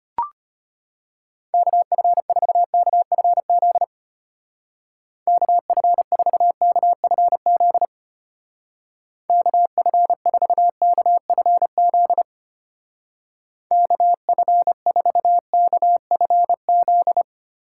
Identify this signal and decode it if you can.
Morse code, heard as a keyed tone, about 30 wpm, 700 Hz. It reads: KF4KFZ KF4KFZ KF4KFZ KF4KFZ
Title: Speed Racing - USA 2x3 Call Signs 25wpm